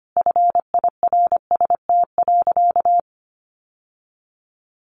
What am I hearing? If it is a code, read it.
Morse code, heard as a keyed tone, about 25 wpm, 700 Hz. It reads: FIRST.
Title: Sentences from Top 800 Words 25wpm